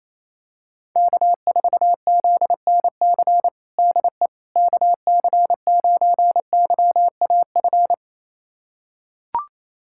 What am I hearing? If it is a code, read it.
Morse code, heard as a keyed tone, about 28 wpm, 700 Hz. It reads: K4ZNC DE KC9YAF
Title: USA Callsign DE Callsign 28wpm